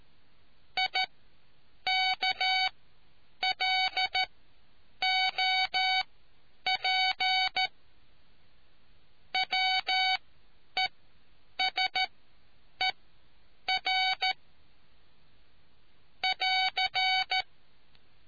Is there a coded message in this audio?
Yes: Morse code from 0.8 to 17.4 s, heard as a keyed tone, about 13 wpm, spaced as for 8 wpm, 750 Hz. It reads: IKLOP WESER <AR>